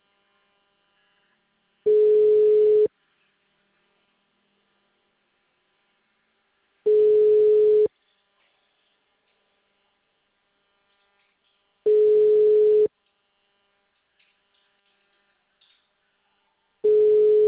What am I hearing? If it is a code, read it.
Ringback tone (Europe, 425 Hz).
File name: out- -2012- - - .9065